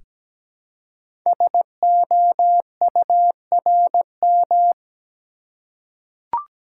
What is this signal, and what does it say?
Morse code, heard as a keyed tone, about 17 wpm, 700 Hz. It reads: SOURM
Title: Code Groups - 5 Letters - 17wpm